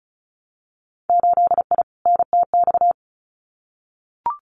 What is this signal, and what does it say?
Morse code, heard as a keyed tone, about 36 wpm, 700 Hz. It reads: 8I NTX
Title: Winter Field Day Exchanges Encore 35wpm